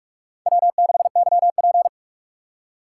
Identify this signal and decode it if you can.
Morse code, heard as a keyed tone, about 44 wpm, 700 Hz. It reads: W6YP